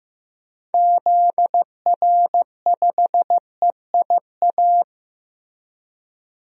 Morse code, heard as a keyed tone, about 15 wpm, 700 Hz. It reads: ZR5EIA
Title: DXCC-100 Call Signs - 6 Characters - 15wpm